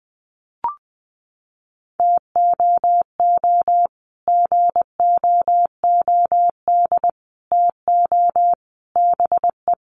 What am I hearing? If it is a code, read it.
Morse code, heard as a keyed tone, about 20 wpm, 700 Hz. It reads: TOO GOOD TO BE
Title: Sets of 5 Words Encore 20wpm